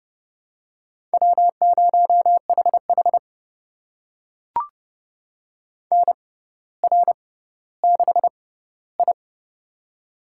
Morse code, heard as a keyed tone, about 30 wpm, spaced as for 15 wpm, 700 Hz. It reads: W0HH NR6I
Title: USA Call Signs - 4 Characters - 6x Character Spacing - 30wpm